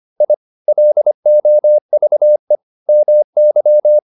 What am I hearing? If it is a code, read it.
Morse code, heard as a keyed tone, about 25 wpm, 600 Hz. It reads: I LOVE MY